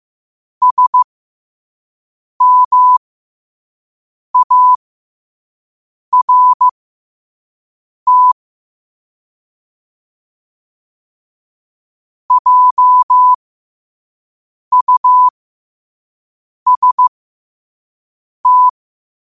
Morse code, heard as a keyed tone, about 15 wpm, spaced as for 5 wpm, 1000 Hz. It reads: SMART JUST
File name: qotdfiles / 5wpm